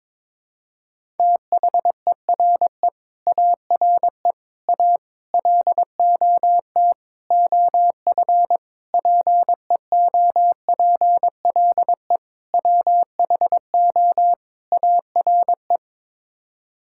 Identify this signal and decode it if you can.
Morse code, heard as a keyed tone, about 22 wpm, 700 Hz. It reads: THERE ARE A LOT OF PEOPLE WHO ARE
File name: practice / 22wpm / Sets of 8 Words - 22wpm